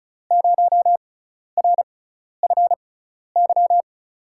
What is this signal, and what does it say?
Morse code, heard as a keyed tone, about 36 wpm, spaced as for 12 wpm, 700 Hz. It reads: 0RFY
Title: USA Call Signs - 6 Characters - 6x Character Spacing - 35wpm